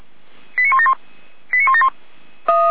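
Five-tone selective call (ZVEI), heard as quick runs of five tones, tones of about 70 ms. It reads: 89171 89171
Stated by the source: Five-tone ZVEI selective call, received by SDR.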